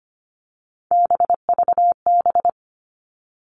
Morse code, heard as a keyed tone, about 25 wpm, 700 Hz. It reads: BVB
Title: All Cut Numbers - 3-Digit 25wpm